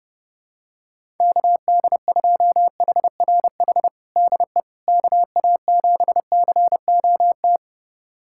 Morse code, heard as a keyed tone, about 30 wpm, 700 Hz. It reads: KD2HRH DE KA7COT
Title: USA Callsign DE Callsign 30wpm